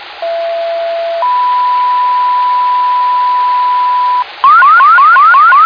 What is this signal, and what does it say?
Two-tone page, heard as a one-second tone, then a three-second tone, about 670 then 990 Hz. It is heard from 0.2 to 4.2 s.